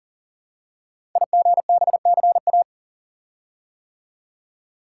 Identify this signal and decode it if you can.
Morse code, heard as a keyed tone, about 38 wpm, 690 Hz. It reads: IGBCA
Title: Code Groups - 5 Letters - 40wpm